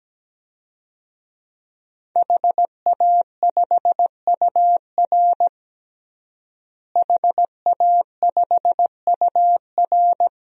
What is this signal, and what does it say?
Morse code, heard as a keyed tone, about 17 wpm, 700 Hz. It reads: HA5UR HA5UR
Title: DXCC-100 Call Signs - 5 Characters - Repeated 2x - 17wpm